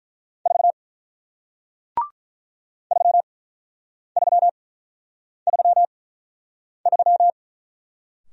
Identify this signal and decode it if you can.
Morse code, heard as a keyed tone, about 48 wpm, spaced as for 9 wpm, 700 Hz. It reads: 4 3333